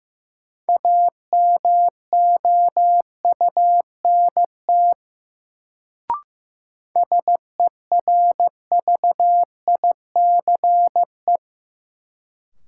Morse code, heard as a keyed tone, about 15 wpm, 700 Hz. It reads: AMOUNT SERVICE
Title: Sentences from Top 700 Words - New Word Drill - 15wpm